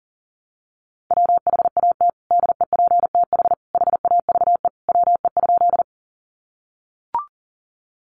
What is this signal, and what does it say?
Morse code, heard as a keyed tone, about 40 wpm, 700 Hz. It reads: WHAT DEPTH HAVE WE?
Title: Sentences from Top 1200 Words 40wpm